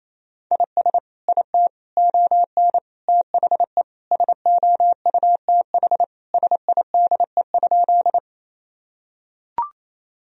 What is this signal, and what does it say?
Morse code, heard as a keyed tone, about 28 wpm, 700 Hz. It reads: IS IT ON THE SOUTH SIDE?
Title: Sentences from Top 1000 Words - Review Part 2 - 28wpm